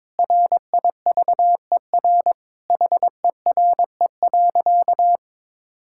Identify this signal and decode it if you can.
Morse code, heard as a keyed tone, about 22 wpm, 700 Hz. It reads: RIVER HERE.
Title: Sentences from Top 1000 Words - Review Part 5 - 22wpm